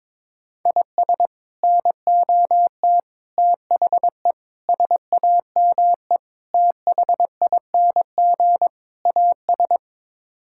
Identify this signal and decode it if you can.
Morse code, heard as a keyed tone, about 22 wpm, 700 Hz. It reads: IS NOT THE SAME THING AS
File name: practice / 22wpm / Sets of 6 Words - 22wpm